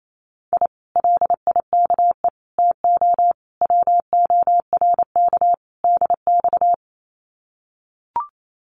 Morse code, heard as a keyed tone, about 28 wpm, 700 Hz. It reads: I LIKE TO WORK DX
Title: Common QSO Phrases Encore 28wpm